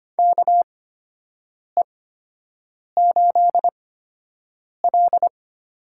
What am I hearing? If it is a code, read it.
Morse code, heard as a keyed tone, about 25 wpm, spaced as for 7 wpm, 700 Hz. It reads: KE8L
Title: USA Call Signs - 6 Characters - 8x Character Spacing - 25wpm